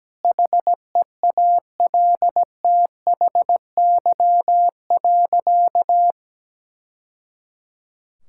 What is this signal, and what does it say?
Morse code, heard as a keyed tone, about 17 wpm, 700 Hz. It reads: HEALTHY.
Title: Sentences from Top 1000 Words - Review Part 6 - 17wpm